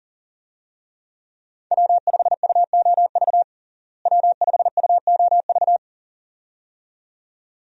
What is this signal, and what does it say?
Morse code, heard as a keyed tone, about 40 wpm, 700 Hz. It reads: W5UOV W5UOV
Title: USA Call Signs - 5 Characters - Repeated 2x - 40wpm